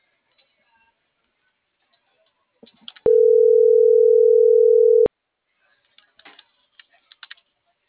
Ringback tone (North America), 440 + 480 Hz.